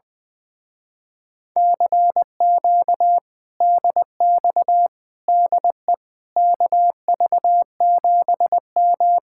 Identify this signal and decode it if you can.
Morse code, heard as a keyed tone, about 20 wpm, 700 Hz. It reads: CQ DX DE KV7M
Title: Speed-Racing Common QSO Phrases - 20wpm